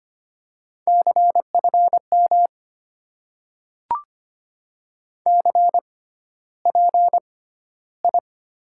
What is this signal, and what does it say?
Morse code, heard as a keyed tone, about 25 wpm, spaced as for 13 wpm, 690 Hz. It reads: CFM CPI
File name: Top 90 QSO Elements - 6x Character Spacing - 25wpm